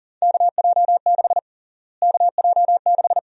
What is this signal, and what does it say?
Morse code, heard as a keyed tone, about 40 wpm, 700 Hz. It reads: KJ6 KJ6
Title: USA Call Sign Prefixes - Repeated 2x - 40wpm